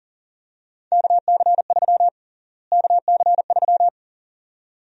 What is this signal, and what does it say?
Morse code, heard as a keyed tone, about 40 wpm, 700 Hz. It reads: KC3 KC3
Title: USA Call Sign Prefixes - Repeated 2x - 40wpm